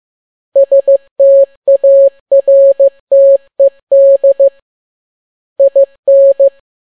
Morse code, heard as a keyed tone, about 15 wpm, 550 Hz. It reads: STARTED IN